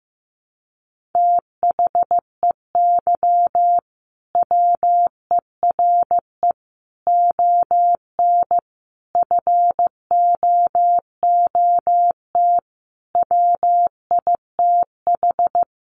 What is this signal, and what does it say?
Morse code, heard as a keyed tone, about 15 wpm, 700 Hz. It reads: THEY WERE ON FOOT WITH
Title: Sentences from Top 1500 Words 15wpm